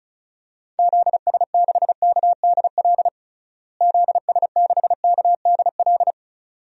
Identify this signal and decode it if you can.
Morse code, heard as a keyed tone, about 36 wpm, 700 Hz. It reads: ZS6KDL ZS6KDL
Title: DXCC-100 Call Signs - 6 Characters - Repeated 2x - 35wpm